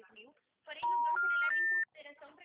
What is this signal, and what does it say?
Special information tone, with speech.